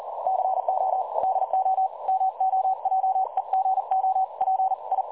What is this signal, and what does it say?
Morse code, heard as a keyed tone, about 38 wpm, 760 Hz. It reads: CO DO MOJEGO PI
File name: dlazaawansowanych / nagraniazpasma / websdr recording 3565.0kHz